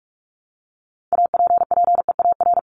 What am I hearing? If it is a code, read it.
Morse code, heard as a keyed tone, about 44 wpm, 700 Hz. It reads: APPEAR
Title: Top 500 Words Repeated 45wpm